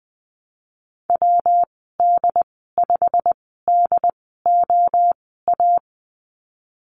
Morse code, heard as a keyed tone, about 20 wpm, spaced as for 15 wpm, 700 Hz. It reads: WD5DOA
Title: 154 Lesson 25g - Letter p - Call Signs 20wpm